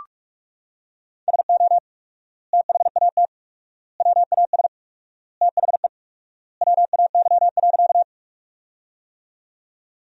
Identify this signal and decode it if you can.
Morse code, heard as a keyed tone, about 46 wpm, 700 Hz. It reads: SO THAT WAS THE WAY.
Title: 4x Word-Spacing - Sentences from Top 100 Words - 45wpm